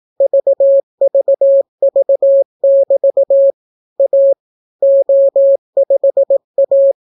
Morse code, heard as a keyed tone, about 18 wpm, 550 Hz. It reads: VVV<BT> A O5A